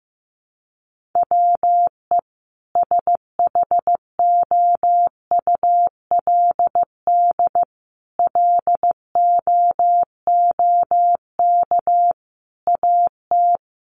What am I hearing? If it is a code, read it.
Morse code, heard as a keyed tone, about 15 wpm, 700 Hz. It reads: WE SHOULD LOOK AT
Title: Sentences from Top 300 Words 15wpm